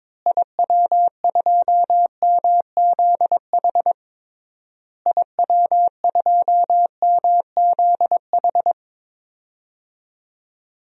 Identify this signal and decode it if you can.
Morse code, heard as a keyed tone, about 22 wpm, 700 Hz. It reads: IW2MZH IW2MZH
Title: DXCC-100 Call Signs - 6 Characters - Repeated 3x - 22wpm